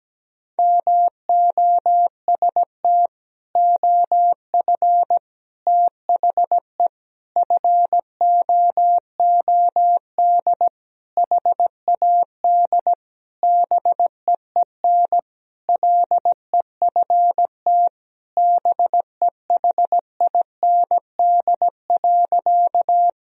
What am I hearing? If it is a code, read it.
Morse code, heard as a keyed tone, about 17 wpm, 700 Hz. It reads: MOST OF THE FOOD HAD BEEN LEFT BEHIND.
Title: Sentences from Top 1000 Words - Review Part 1 - 17wpm